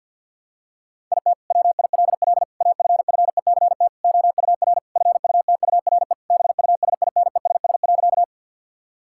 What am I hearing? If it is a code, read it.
Morse code, heard as a keyed tone, about 50 wpm, 700 Hz. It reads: IT WILL AFFECT OUR FUTURE BUSINESS.
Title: Sentences from Top 1000 Words - Review Part 6 - 50wpm